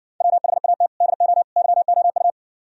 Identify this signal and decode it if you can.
Morse code, heard as a keyed tone, about 59 wpm, 700 Hz. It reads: WHAT DO YOU